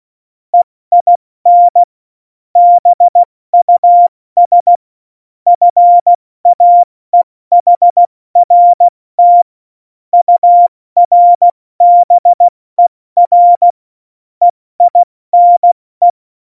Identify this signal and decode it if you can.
Morse code, heard as a keyed tone, about 16 wpm, 700 Hz. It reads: EIN BUS FAEHRT URBER EINE
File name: Morsekurs / Witze / Witz 29. Januar 2019, 70 F80